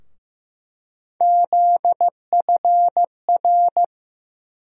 Morse code, heard as a keyed tone, about 15 wpm, 700 Hz. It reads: ZFR
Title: USA Call Sign Postfixes - 3 Letters 15wpm